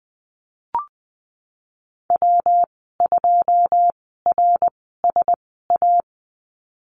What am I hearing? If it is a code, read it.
Morse code, heard as a keyed tone, about 19 wpm, spaced as for 14 wpm, 700 Hz. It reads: W2RSA
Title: 140 Lesson 23g - Letter f - Call Signs 20wpm